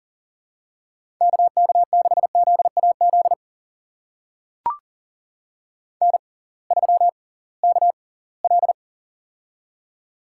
Morse code, heard as a keyed tone, about 40 wpm, spaced as for 23 wpm, 700 Hz. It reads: KK6ZAZ N3KL
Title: USA Call Signs - 6x Character Spacing - 40wpm